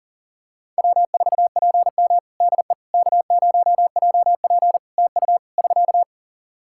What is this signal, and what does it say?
Morse code, heard as a keyed tone, about 38 wpm, 700 Hz. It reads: W4PM DE K0JP TU <SK>
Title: Practice QSO - Exchange 4 - Ending - 40wpm